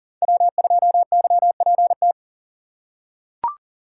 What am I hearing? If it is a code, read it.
Morse code, heard as a keyed tone, about 40 wpm, 700 Hz. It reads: W2YPT
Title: USA Call Signs - 2x Character Spacing - 40wpm